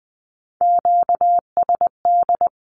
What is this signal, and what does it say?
Morse code, heard as a keyed tone, about 20 wpm, 700 Hz. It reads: QSD
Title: Q-Codes with Spoken Meaning - 20wpm